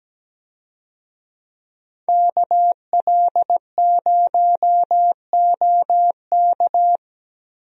Morse code, heard as a keyed tone, about 17 wpm, 700 Hz. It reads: KL0OK
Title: USA Call Signs - 5 Characters - Repeated 2x - 17wpm